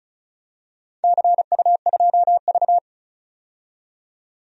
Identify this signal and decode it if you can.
Morse code, heard as a keyed tone, about 36 wpm, 700 Hz. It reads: CU2V
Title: DXCC-100 Call Signs - 4 Characters - 35wpm